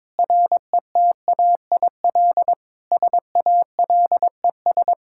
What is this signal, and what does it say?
Morse code, heard as a keyed tone, about 21 wpm, 700 Hz. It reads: RETAIL SALES
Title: Practice QSO - Exchange 3 - More Info - 22wpm